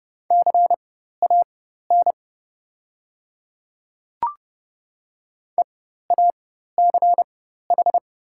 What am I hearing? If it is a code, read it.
Morse code, heard as a keyed tone, about 30 wpm, spaced as for 14 wpm, 700 Hz. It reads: CAN EACH
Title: 4x Character Spacing - Top 500 Words - 30wpm